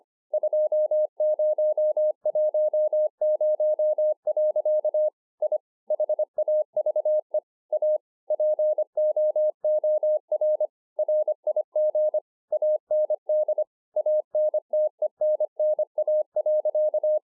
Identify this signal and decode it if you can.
Morse code, heard as a keyed tone, about 24 wpm, 610 Hz. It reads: 2010. I HAVE A POOR RIG AND ANTENNA.
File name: 2011-04-mp3 / BEGINNERS DIARY 25WPM